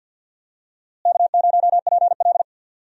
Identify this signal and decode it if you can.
Morse code, heard as a keyed tone, about 47 wpm, 700 Hz. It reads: K0PL
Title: USA Call Signs - 4 Characters - Repeated 3x - 50wpm